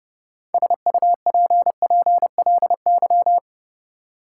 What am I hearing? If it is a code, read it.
Morse code, heard as a keyed tone, about 32 wpm, 700 Hz. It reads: SUPPLY
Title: Sentences from Top 1200 Words - New Word Drill Alt - 30wpm